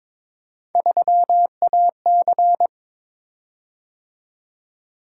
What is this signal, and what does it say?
Morse code, heard as a keyed tone, about 22 wpm, 700 Hz. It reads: 3AC
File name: practice / 22wpm / Three Letter-Number Combinations - 22wpm